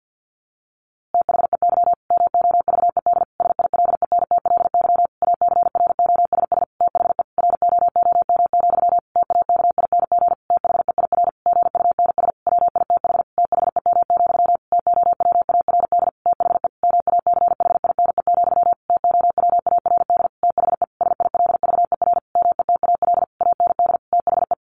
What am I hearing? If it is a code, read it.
Morse code, heard as a keyed tone, about 50 wpm, 700 Hz. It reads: THEY MOVED SILENTLY ACROSS THE ROOM, TAKING THEIR GUNS WITH THEM, TOWARD THE MACHINE, TOWARD THE SILVER METAL AND THE
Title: A Sound Of Thunder 50wpm